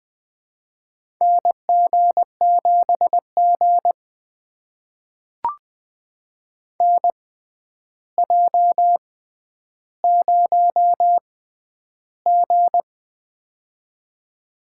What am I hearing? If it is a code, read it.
Morse code, heard as a keyed tone, about 21 wpm, spaced as for 10 wpm, 700 Hz. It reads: NG7G NJ0G
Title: USA Call Signs - 4 Characters - 6x Character Spacing - 20wpm